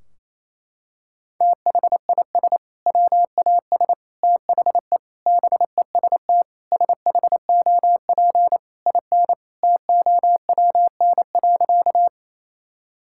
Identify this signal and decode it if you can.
Morse code, heard as a keyed tone, about 28 wpm, 700 Hz. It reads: THIS WAS THE BEST SHOP IN TOWN.
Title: Sentences from Top 1600 Words 28wpm